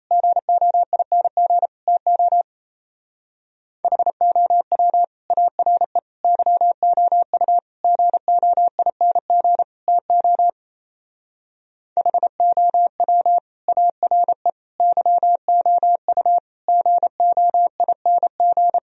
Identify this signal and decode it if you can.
Morse code, heard as a keyed tone, about 32 wpm, 700 Hz. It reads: GOING TO HOW ARE YOU GOING TO HOW ARE YOU GOING
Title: Speed-Racing - Sets of 5 Words 28wpm